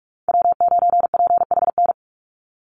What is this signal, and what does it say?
Morse code, heard as a keyed tone, about 44 wpm, 700 Hz. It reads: W9PHN
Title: USA Call Signs 45wpm